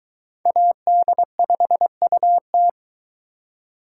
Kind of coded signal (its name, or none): Morse code